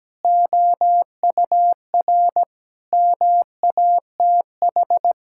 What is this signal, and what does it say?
Morse code, heard as a keyed tone, about 17 wpm, 700 Hz. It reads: OUR MATH
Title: Sentences from Top 1000 Words - Review Part 6 - 17wpm